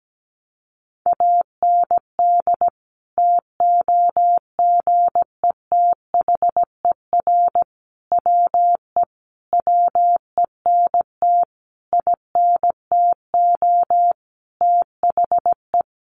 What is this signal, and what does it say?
Morse code, heard as a keyed tone, about 17 wpm, 700 Hz. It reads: AND TOGETHER WE WENT INTO THE
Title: Sentences from Top 600 Words 17wpm